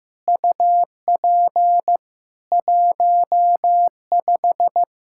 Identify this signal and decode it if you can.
Morse code, heard as a keyed tone, about 15 wpm, 700 Hz. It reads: UP 15
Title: Practice QSO - Exchange 2 - Info - 15wpm